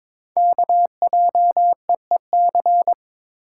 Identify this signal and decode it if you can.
Morse code, heard as a keyed tone, about 22 wpm, 700 Hz. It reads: KJEEC
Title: Code Groups - 5 Letters - 22wpm